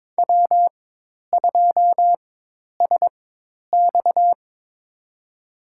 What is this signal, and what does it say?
Morse code, heard as a keyed tone, about 22 wpm, spaced as for 10 wpm, 700 Hz. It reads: W2SX
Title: USA Call Signs - 4 Characters - 4x Character Spacing - 22wpm